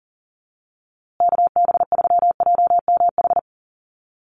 Morse code, heard as a keyed tone, about 40 wpm, 700 Hz. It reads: KB3JMH
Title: USA Call Signs Repeated 3x 40wpm